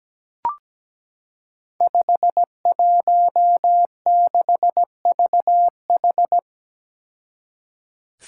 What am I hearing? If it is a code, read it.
Morse code, heard as a keyed tone, about 17 wpm, 700 Hz. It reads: 516VH
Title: Five Challenging Letter-Number Combinations 17wpm